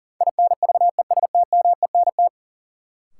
Morse code, heard as a keyed tone, about 40 wpm, 700 Hz. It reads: INVESTMENT
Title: Sentences from Top 700 Words - New Word Drill - 40wpm